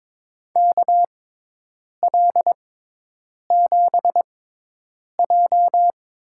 Morse code, heard as a keyed tone, about 22 wpm, spaced as for 8 wpm, 700 Hz. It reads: KL7J
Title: USA Call Signs - 6 Characters - 6x Character Spacing - 22wpm